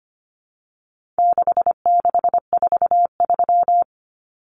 Morse code, heard as a keyed tone, about 25 wpm, 700 Hz. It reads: 6643
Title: Common Cut Numbers - 4-Digit 25wpm